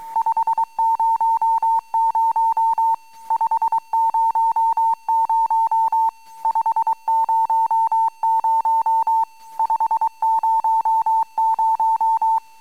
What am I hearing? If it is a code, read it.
Morse code, heard as a keyed tone, about 23 wpm, 910 Hz. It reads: 500 500 500 500
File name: amsat / cw